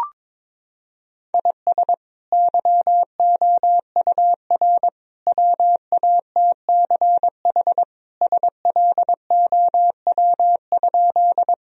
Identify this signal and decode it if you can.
Morse code, heard as a keyed tone, about 22 wpm, 700 Hz. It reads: IS YOUR WATCH SLOW?
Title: Sentences from Top 1000 Words - Review Part 2 - 22wpm